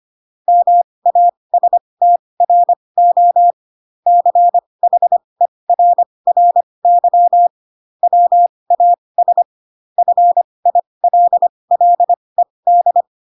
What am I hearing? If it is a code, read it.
Morse code, heard as a keyed tone, about 25 wpm, spaced as for 20 wpm, 700 Hz. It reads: MASTRO CHERRY WAS FILLED
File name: Pinocchio / Pinocchio ch1 15 25 ik1hge